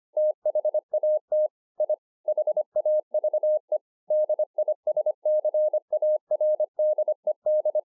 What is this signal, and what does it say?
Morse code, heard as a keyed tone, about 25 wpm, 610 Hz. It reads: THAT I HAVE DISCARDED